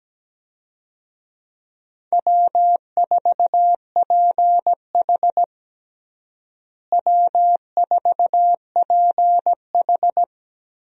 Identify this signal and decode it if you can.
Morse code, heard as a keyed tone, about 17 wpm, 700 Hz. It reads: W4PH W4PH
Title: USA Call Signs - 4 Characters - Repeated 2x - 17wpm